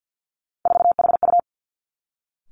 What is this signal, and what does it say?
Morse code, heard as a keyed tone, about 50 wpm, 700 Hz. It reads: 4HU
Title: Three Challenging Letter-Number Combinations 50wpm